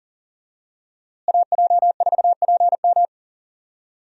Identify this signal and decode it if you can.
Morse code, heard as a keyed tone, about 37 wpm, 700 Hz. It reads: AJ4PM
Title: USA Call Signs - 5 Characters - Repeated 3x - 40wpm